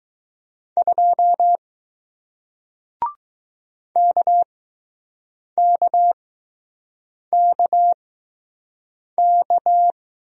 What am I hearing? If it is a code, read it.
Morse code, heard as a keyed tone, about 20 wpm, spaced as for 6 wpm, 700 Hz. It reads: 2 KKKK